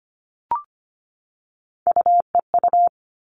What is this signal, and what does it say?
Morse code, heard as a keyed tone, about 25 wpm, 700 Hz. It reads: UEU
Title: All Cut Numbers - 3-Digit 25wpm